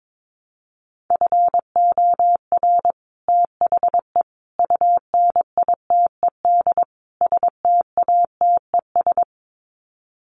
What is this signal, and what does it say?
Morse code, heard as a keyed tone, about 22 wpm, 700 Hz. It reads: FOR THE UNITED STATES
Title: Sets of 4 Words 22wpm